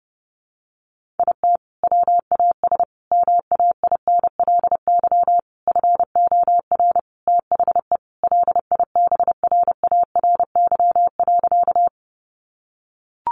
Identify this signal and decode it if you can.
Morse code, heard as a keyed tone, about 30 wpm, 700 Hz. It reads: IT WAS MAINLY FOR THE LIBRARY.
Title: Sentences from Top 1400 Words 30wpm